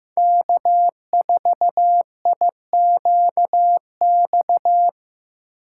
Morse code, heard as a keyed tone, about 15 wpm, 700 Hz. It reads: K4IQX
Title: USA Call Signs - 5 Characters - 2x Character Spacing - 15wpm